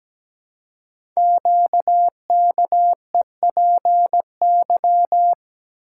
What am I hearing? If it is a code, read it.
Morse code, heard as a keyed tone, about 17 wpm, 700 Hz. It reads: QKEPY
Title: Code Groups - 5 Letters - 17wpm